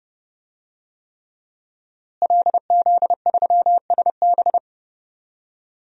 Morse code, heard as a keyed tone, about 28 wpm, 700 Hz. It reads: LZ3SB
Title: DXCC-100 Call Signs - 5 Characters - Repeated 3x - 30wpm